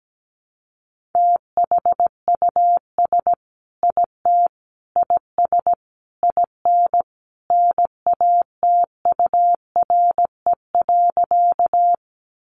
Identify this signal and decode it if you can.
Morse code, heard as a keyed tone, about 17 wpm, 700 Hz. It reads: THUS IT IS IN NATURE.
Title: Sentences from Top 800 Words 17wpm